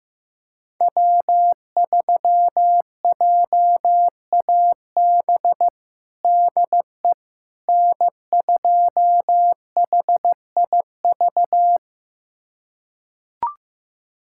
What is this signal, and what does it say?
Morse code, heard as a keyed tone, about 16 wpm, 700 Hz. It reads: W3JAB DE N2HIV
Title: USA Callsign DE Callsign - 5 Characters - 15wpm